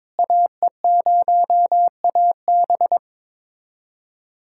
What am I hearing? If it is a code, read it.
Morse code, heard as a keyed tone, about 22 wpm, 700 Hz. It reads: AE0AB
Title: USA Callsign DE Callsign - 5 Characters - 22wpm